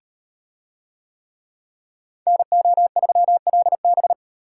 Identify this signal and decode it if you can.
Morse code, heard as a keyed tone, about 38 wpm, 700 Hz. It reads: NO3LB